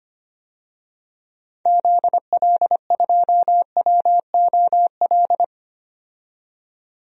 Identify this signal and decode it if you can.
Morse code, heard as a keyed tone, about 25 wpm, 700 Hz. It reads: ZL2WOL
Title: DXCC-100 Call Signs - 6 Characters - 25wpm